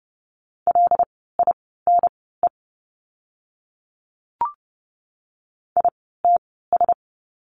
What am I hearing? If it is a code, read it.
Morse code, heard as a keyed tone, about 30 wpm, spaced as for 17 wpm, 700 Hz. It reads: LINE ITS
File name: Lesson 16f - Letter c - Words 30wpm